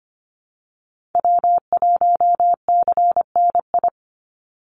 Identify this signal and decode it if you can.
Morse code, heard as a keyed tone, about 25 wpm, 700 Hz. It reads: W1CNI
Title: USA Call Signs 25wpm